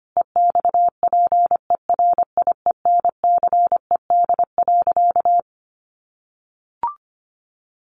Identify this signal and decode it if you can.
Morse code, heard as a keyed tone, about 25 wpm, 700 Hz. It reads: EXPERIENCED.
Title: Sentences from Top 1100 Words 25wpm